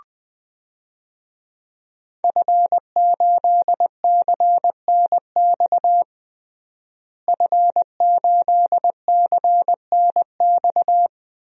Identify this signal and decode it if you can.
Morse code, heard as a keyed tone, about 20 wpm, 700 Hz. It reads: F8CNX F8CNX
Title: DXCC-100 Call Signs - 5 Characters - Repeated 2x - 20wpm